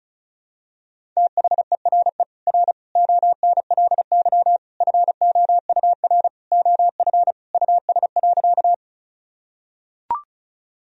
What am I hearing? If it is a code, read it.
Morse code, heard as a keyed tone, about 34 wpm, 700 Hz. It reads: THERE R ONLY FOUR OF US.
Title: Sentences from Top 300 Words and Abbreviations 35wpm